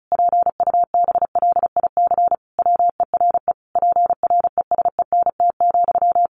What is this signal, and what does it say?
Morse code, heard as a keyed tone, about 34 wpm, 700 Hz. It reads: PUBLIC WERE PRESENT,
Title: Sentences from Top 800 Words 35wpm